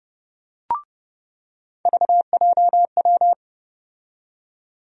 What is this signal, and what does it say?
Morse code, heard as a keyed tone, about 30 wpm, 700 Hz. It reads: VJW
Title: USA Call Sign Postfixes - 3 Letters 30wpm